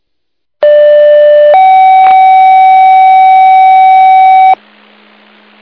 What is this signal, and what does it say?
Two-tone page, heard as a one-second tone, then a three-second tone, about 590 then 760 Hz, with noise. It is heard from 0.6 to 4.5 s.